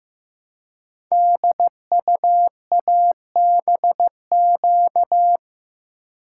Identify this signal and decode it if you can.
Morse code, heard as a keyed tone, about 15 wpm, 700 Hz. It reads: DUABQ
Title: Code Groups - 5 Letters - 15wpm